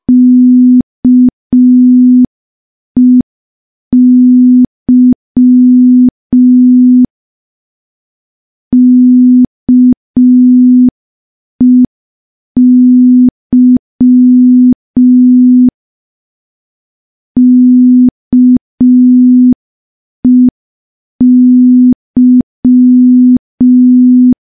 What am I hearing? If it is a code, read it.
Morse code, heard as a keyed tone, about 5 wpm, 250 Hz. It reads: KEY KEY KEY
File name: morse / morse-training-05WPM